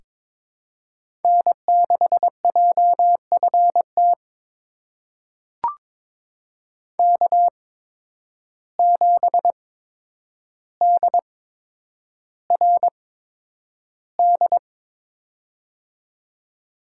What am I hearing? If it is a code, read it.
Morse code, heard as a keyed tone, about 22 wpm, spaced as for 9 wpm, 700 Hz. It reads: N6JFT K7DRD
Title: USA Call Signs - 5 Characters - 8x Character Spacing - 22wpm